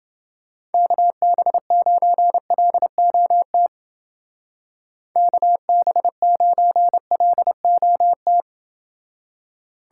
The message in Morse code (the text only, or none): KB9LOT KB9LOT